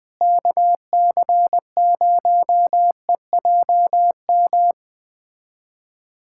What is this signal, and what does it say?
Morse code, heard as a keyed tone, about 20 wpm, 700 Hz. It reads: KC0EJM
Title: USA Callsign DE Callsign 20wpm